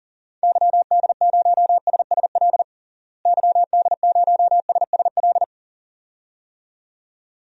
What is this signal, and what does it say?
Morse code, heard as a keyed tone, about 40 wpm, 700 Hz. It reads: YD0SSL YD0SSL
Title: DXCC-100 Call Signs - 6 Characters - Repeated 2x - 40wpm